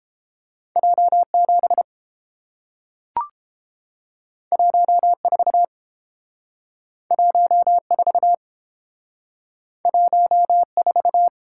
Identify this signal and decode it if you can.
Morse code, heard as a keyed tone, about 29 wpm, 700 Hz. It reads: J7 14 14 14